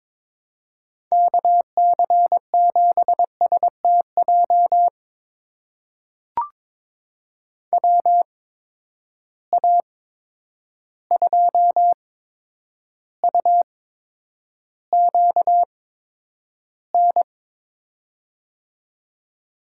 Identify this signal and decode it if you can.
Morse code, heard as a keyed tone, about 22 wpm, spaced as for 9 wpm, 700 Hz. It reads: KC7STJ WA2UQN
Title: USA Call Signs - 6 Characters - 8x Character Spacing - 22wpm